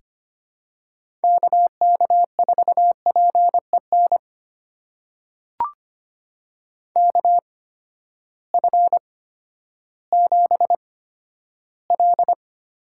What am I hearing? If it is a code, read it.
Morse code, heard as a keyed tone, about 25 wpm, spaced as for 12 wpm, 700 Hz. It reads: KK4PEN KF7L
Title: USA Call Signs - 6 Characters - 8x Character Spacing - 25wpm